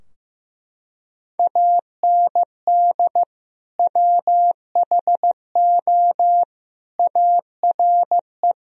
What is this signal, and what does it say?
Morse code, heard as a keyed tone, about 15 wpm, 700 Hz. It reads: AND WHO ARE
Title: Sentences from Top 1000 Words - Review Part 2 - 15wpm